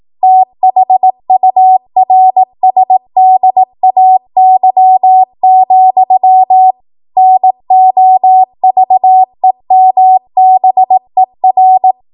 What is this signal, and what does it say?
Morse code, heard as a keyed tone, about 18 wpm, 750 Hz. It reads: THURSDAY, NOVEMBER
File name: Archive / 18 WPM / 18WPM